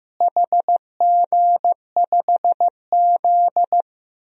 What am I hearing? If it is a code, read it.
Morse code, heard as a keyed tone, about 15 wpm, 700 Hz. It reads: HG5Z
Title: DXCC-100 Call Signs - 4 Characters - Repeated 2x - 15wpm